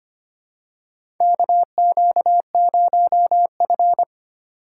Morse code, heard as a keyed tone, about 25 wpm, 700 Hz. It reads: KQ0F
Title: USA Call Signs - 4 Characters - 3x Character Spacing - 25wpm